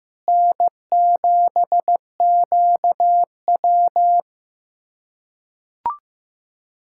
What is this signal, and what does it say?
Morse code, heard as a keyed tone, about 15 wpm, 700 Hz. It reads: N7QW
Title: USA Callsign DE Callsign - 4 Characters - 15wpm